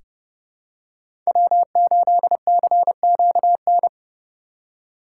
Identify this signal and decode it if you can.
Morse code, heard as a keyed tone, about 31 wpm, 700 Hz. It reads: W8CQN
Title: USA Call Signs - 5 Characters - 2x Character Spacing - 30wpm